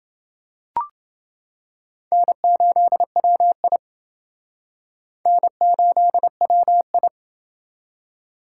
Morse code, heard as a keyed tone, about 29 wpm, 700 Hz. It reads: N8WI N8WI